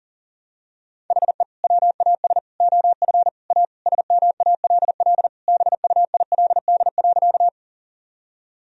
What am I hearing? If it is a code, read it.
Morse code, heard as a keyed tone, about 40 wpm, 700 Hz. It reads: HE WAS OF A SMALL BUILD.
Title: Sentences from Top 1000 Words - Review Part 1 - 40wpm